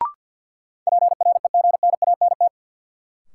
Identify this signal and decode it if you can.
Morse code, heard as a keyed tone, about 52 wpm, 700 Hz. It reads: PREGNANT
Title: Sentences from Top 1000 Words - Review Part 6 - New Word Drill - 50wpm